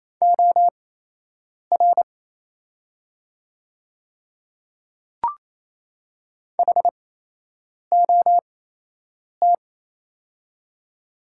Morse code, heard as a keyed tone, about 27 wpm, spaced as for 8 wpm, 700 Hz. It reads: OR HOT
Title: 8x Character Spacing - Top 300 Words - 28wpm